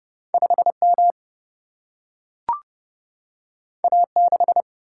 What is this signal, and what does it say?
Morse code, heard as a keyed tone, about 30 wpm, 700 Hz. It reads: 5 M A 6
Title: Single-Letter Single-Number 30wpm